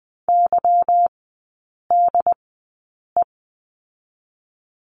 Morse code, heard as a keyed tone, about 20 wpm, spaced as for 8 wpm, 700 Hz. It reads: YDE
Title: 145 Lesson 24e - Letter y - Three Characters 20wpm